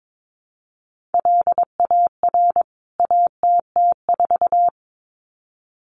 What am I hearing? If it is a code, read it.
Morse code, heard as a keyed tone, about 22 wpm, 700 Hz. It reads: LAR ATT4
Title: CWT Exchange 22wpm